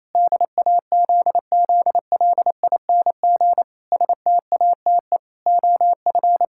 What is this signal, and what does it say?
Morse code, heard as a keyed tone, about 28 wpm, 700 Hz. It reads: DAZZLING STATE OF
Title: Pangram Challenge v2 - 28wpm